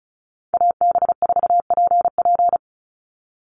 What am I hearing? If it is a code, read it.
Morse code, heard as a keyed tone, about 34 wpm, 700 Hz. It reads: AB4PP
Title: NA SOTA Chaser Callsigns 35wpm